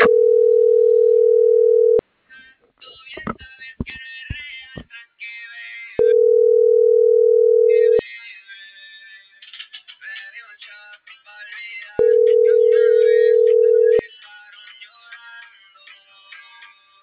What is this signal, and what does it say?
Ringback tone (North America), 440 + 480 Hz.